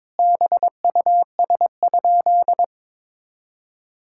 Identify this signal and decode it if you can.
Morse code, heard as a keyed tone, about 22 wpm, 700 Hz. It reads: BUS?
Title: Sentences from Top 1000 Words - Review Part 6 - 22wpm